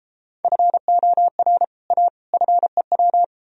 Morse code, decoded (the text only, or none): FOR A FEW